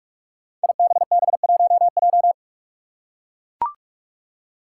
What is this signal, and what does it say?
Morse code, heard as a keyed tone, about 44 wpm, 700 Hz. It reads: IBB1J
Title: Five Challenging Letter-Number Combinations 45wpm